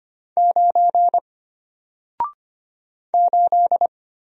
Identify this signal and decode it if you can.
Morse code, heard as a keyed tone, about 25 wpm, 700 Hz. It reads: 9 8
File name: Single-Letter-Number-25wpm